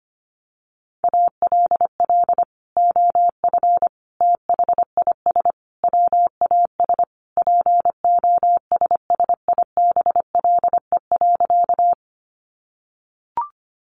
Morse code, heard as a keyed tone, about 25 wpm, 700 Hz. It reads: ALL OF THIS WAS POSSIBLE.
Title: Sentences from Top 500 Words 25wpm